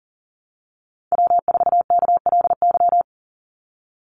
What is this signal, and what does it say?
Morse code, heard as a keyed tone, about 40 wpm, 700 Hz. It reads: W4KLY
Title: Call Signs - 40wpm